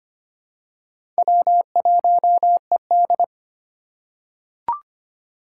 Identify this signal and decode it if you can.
Morse code, heard as a keyed tone, about 25 wpm, 700 Hz. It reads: W1ED
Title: USA Call Signs - 4 Characters - 3x Character Spacing - 25wpm